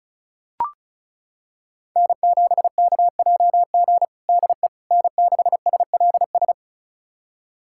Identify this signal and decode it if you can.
Morse code, heard as a keyed tone, about 34 wpm, 700 Hz. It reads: N7KJG DE N6SLS
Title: USA Callsign DE Callsign - 5 Characters - 35wpm